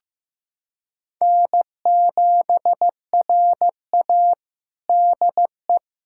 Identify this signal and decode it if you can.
Morse code, heard as a keyed tone, about 15 wpm, 700 Hz. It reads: N7RA DE
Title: USA Callsign DE Callsign - 4 Characters - 15wpm